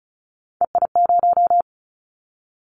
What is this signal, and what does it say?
Morse code, heard as a keyed tone, about 34 wpm, 700 Hz. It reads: EI0
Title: Callsign Prefixes for Top 50 DXCC - 35wpm